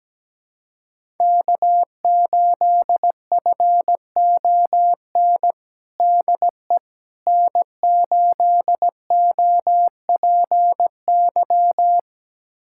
Morse code, heard as a keyed tone, about 17 wpm, 700 Hz. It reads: K8FON DE N8OPY
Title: USA Callsign DE Callsign - 5 Characters - 17wpm